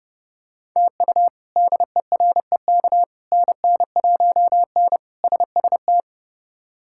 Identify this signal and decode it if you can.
Morse code, heard as a keyed tone, about 30 wpm, 700 Hz. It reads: TU DEREK NN1N SST
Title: K1USN SST QSO Practice - 30wpm